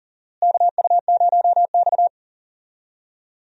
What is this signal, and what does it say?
Morse code, heard as a keyed tone, about 40 wpm, 700 Hz. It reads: KU0X
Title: USA Call Signs - 4 Characters - Repeated 2x - 40wpm